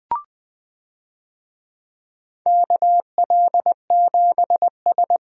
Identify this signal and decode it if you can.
Morse code, heard as a keyed tone, about 20 wpm, 700 Hz. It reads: KL7S
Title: DXCC-100 Call Signs - 4 Characters - 20wpm